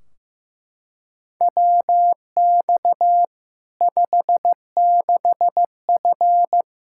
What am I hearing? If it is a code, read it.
Morse code, heard as a keyed tone, about 15 wpm, 700 Hz. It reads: WX 56F
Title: Practice QSO - Exchange 2 - Info - 15wpm